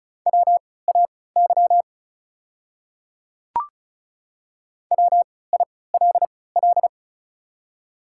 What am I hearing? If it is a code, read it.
Morse code, heard as a keyed tone, about 33 wpm, spaced as for 19 wpm, 690 Hz. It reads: WAY WILL
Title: 3x Character Spacing - Top 100 Words - 35wpm